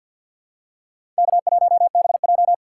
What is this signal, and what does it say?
Morse code, heard as a keyed tone, about 52 wpm, 700 Hz. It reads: K1BJ
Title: USA Call Signs - 4 Characters - 2x Character Spacing - 50wpm